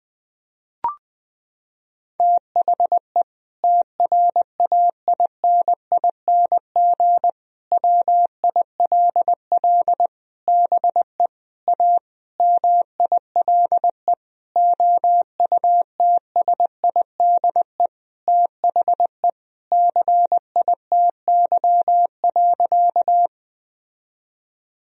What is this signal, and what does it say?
Morse code, heard as a keyed tone, about 20 wpm, 700 Hz. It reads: THE TRAINING WILL BE A MILE OUTSIDE THE CITY.
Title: Sentences from Top 1000 Words - Review Part 4 - 20wpm